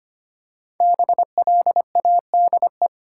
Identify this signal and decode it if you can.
Morse code, heard as a keyed tone, about 25 wpm, 700 Hz. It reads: BLADE
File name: 5 Letter Words - First 3 Letters Given - 25wpm